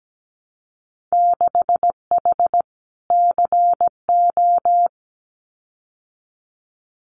Morse code, heard as a keyed tone, about 17 wpm, 700 Hz. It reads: 6H CO
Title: Winter Field Day Exchanges Encore 17wpm